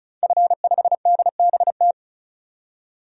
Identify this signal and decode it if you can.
Morse code, heard as a keyed tone, about 36 wpm, 700 Hz. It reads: F5DBT
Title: DXCC-100 Call Signs - 5 Characters - 35wpm